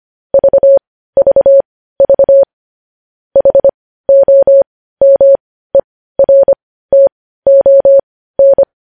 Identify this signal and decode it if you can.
Morse code, heard as a keyed tone, about 25 wpm, spaced as for 15 wpm, 550 Hz. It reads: VVV HOMERTON
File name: podcast / News headlines-1x-25x15